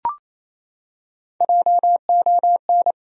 Morse code, heard as a keyed tone, about 27 wpm, 700 Hz. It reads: JON
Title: USA Common Names - Max 4 Letters - 28wpm